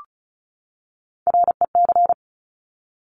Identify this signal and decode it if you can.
Morse code, heard as a keyed tone, about 36 wpm, 700 Hz. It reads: REC
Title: Top Three Letter Combinations in Words 35wpm